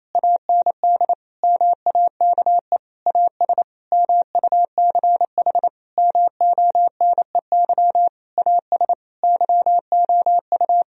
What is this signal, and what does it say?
Morse code, heard as a keyed tone, about 28 wpm, 700 Hz. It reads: AND MAKE AS MUCH MONEY AS YOU
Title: Sentences from Top 1000 Words - Review Part 3 - 28wpm